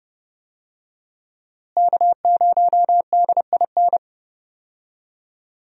Morse code, heard as a keyed tone, about 29 wpm, 700 Hz. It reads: K0DIN
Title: USA Call Signs - 5 Characters - Repeated 3x - 30wpm